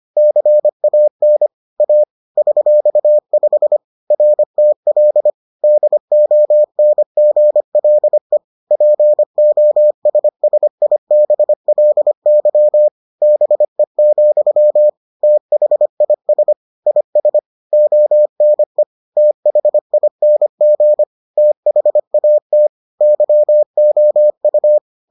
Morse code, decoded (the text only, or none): CAN A $5 RTL DONGLE POSSIBLY BE, THIS IS ONE THING THAT YOU